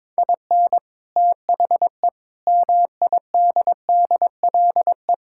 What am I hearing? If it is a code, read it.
Morse code, heard as a keyed tone, about 22 wpm, 700 Hz. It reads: IN THE MIDDLE
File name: practice / 22wpm / Sentences with 3-Word Phrases - 22wpm